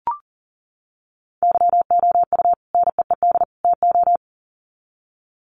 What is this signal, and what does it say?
Morse code, heard as a keyed tone, about 37 wpm, 700 Hz. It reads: YOU NEED TO
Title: Sets of 3 Words 40wpm